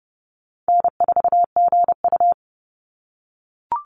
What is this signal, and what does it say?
Morse code, heard as a keyed tone, about 30 wpm, 700 Hz. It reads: N4GU
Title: Call Signs - 30wpm